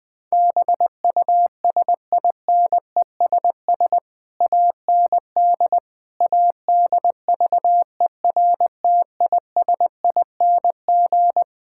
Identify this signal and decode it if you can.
Morse code, heard as a keyed tone, about 20 wpm, 700 Hz. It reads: BUSINESS AND ADVERTISING
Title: Sentences from Top 1000 Words - Review Part 6 - 20wpm